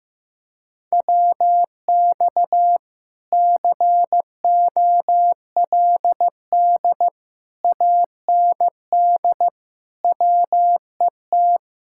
Morse code, heard as a keyed tone, about 15 wpm, 700 Hz. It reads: WX COLD AND WET
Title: Speed-Racing Common QSO Phrases Encore - 15wpm